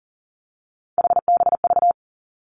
Morse code, heard as a keyed tone, about 41 wpm, 700 Hz. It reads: HBV
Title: Three Challenging Letter-Number Combinations 40wpm